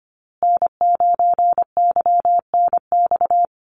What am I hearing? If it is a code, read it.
Morse code, heard as a keyed tone, about 25 wpm, 700 Hz. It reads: N9YNX
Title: USA Call Signs Repeated 2x 25wpm